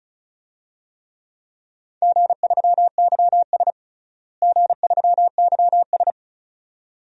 Morse code, heard as a keyed tone, about 36 wpm, 700 Hz. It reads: G3YS G3YS
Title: DXCC-100 Call Signs - 4 Characters - Repeated 2x - 35wpm